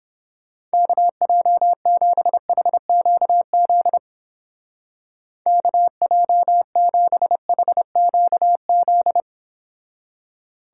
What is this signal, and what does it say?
Morse code, heard as a keyed tone, about 29 wpm, 700 Hz. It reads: KJ7HQZ KJ7HQZ